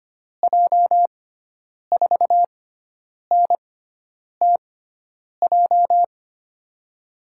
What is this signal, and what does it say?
Morse code, heard as a keyed tone, about 25 wpm, spaced as for 9 wpm, 700 Hz. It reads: J4NTJ
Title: USA Call Signs - 6x Character Spacing - 25wpm